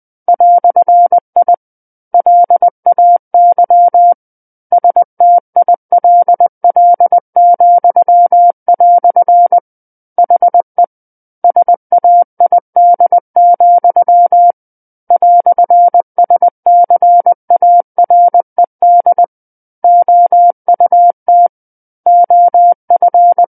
Morse code, heard as a keyed tone, about 20 wpm, 700 Hz. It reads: "I LAY STILL," HE SAID, "SCARED OUT OF
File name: 20 WPM